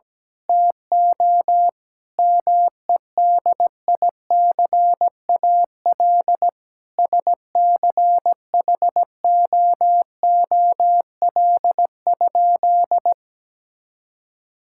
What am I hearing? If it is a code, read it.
Morse code, heard as a keyed tone, about 17 wpm, 700 Hz. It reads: TO MEDICAL SCHOOL?
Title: Sentences from Top 1000 Words - Review Part 5 - 17wpm